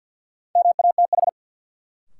Morse code, heard as a keyed tone, about 53 wpm, 700 Hz. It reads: MATH
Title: Sentences from Top 1000 Words - Review Part 6 - New Word Drill - 50wpm